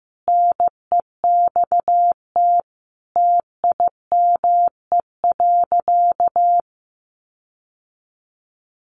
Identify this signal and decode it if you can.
Morse code, heard as a keyed tone, about 14 wpm, 700 Hz. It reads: NEXT TIME.
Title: Sentences from Top 1500 Words 15wpm